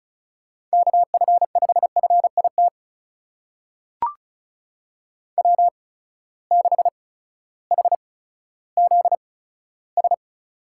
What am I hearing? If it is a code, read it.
Morse code, heard as a keyed tone, about 36 wpm, spaced as for 16 wpm, 700 Hz. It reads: KF5FIT W6HZS
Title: USA Call Signs - 8x Character Spacing - 35wpm